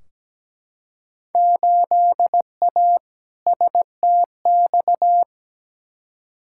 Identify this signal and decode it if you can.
Morse code, heard as a keyed tone, about 17 wpm, 700 Hz. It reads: 8A STX
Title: Field Day Exchanges Encore - 17wpm